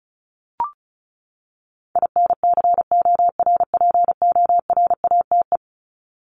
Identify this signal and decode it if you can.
Morse code, heard as a keyed tone, about 34 wpm, 700 Hz. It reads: INCORPORATE
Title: Top 1751-2000 Words 35wpm